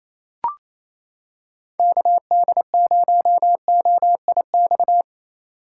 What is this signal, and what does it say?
Morse code, heard as a keyed tone, about 28 wpm, 700 Hz. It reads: KD0OIX
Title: K1USN SST QSO Practice - 28wpm